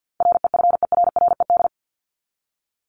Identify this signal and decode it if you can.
Morse code, heard as a keyed tone, about 50 wpm, 700 Hz. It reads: REFERRED
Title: ED Suffix - Top 100 Words with Base Words 50wpm